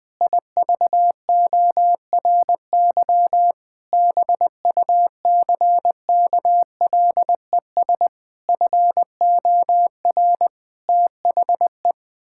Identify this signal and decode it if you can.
Morse code, heard as a keyed tone, about 21 wpm, 700 Hz. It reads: IVORY BUCKLES FOR THE
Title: Pangram Challenge v2 - 20wpm